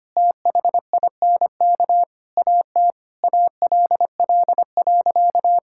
Morse code, heard as a keyed tone, about 25 wpm, 700 Hz. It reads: THINK AT ALL.
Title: Sentences from Top 1600 Words 25wpm